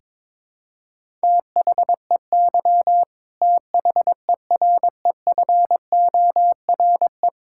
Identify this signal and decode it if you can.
Morse code, heard as a keyed tone, about 22 wpm, 700 Hz. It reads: THEY THEREFORE
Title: Sentences from Top 1000 Words - Review Part 5 - 22wpm